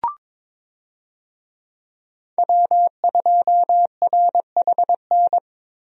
Morse code, heard as a keyed tone, about 22 wpm, 700 Hz. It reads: W2RHN
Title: USA Call Signs - 5 Characters - Repeated 2x - 22wpm